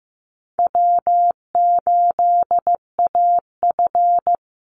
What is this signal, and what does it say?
Morse code, heard as a keyed tone, about 15 wpm, 700 Hz. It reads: W8AF
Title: USA Call Signs Repeated 2x 15wpm